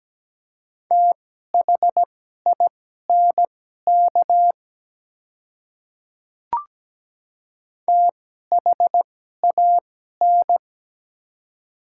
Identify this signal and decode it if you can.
Morse code, heard as a keyed tone, about 17 wpm, spaced as for 12 wpm, 700 Hz. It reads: THINK THAN
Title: 2x Character Spacing - Top 200 Words - 17wpm